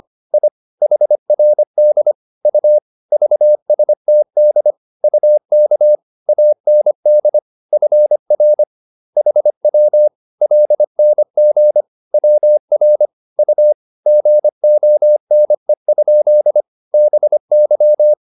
Morse code, heard as a keyed tone, about 25 wpm, 600 Hz. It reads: I HRD U VSTD UK AND FR HW LNG WR U GONE? BY